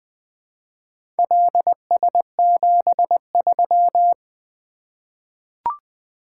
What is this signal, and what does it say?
Morse code, heard as a keyed tone, about 20 wpm, 700 Hz. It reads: LS73
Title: Four Letter-Number Combinations - 20wpm